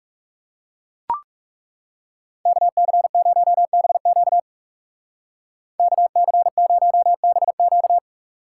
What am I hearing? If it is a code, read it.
Morse code, heard as a keyed tone, about 37 wpm, 700 Hz. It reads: KC0BQ KC0BQ
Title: Speed Racing - USA 2x2 Call Signs 30wpm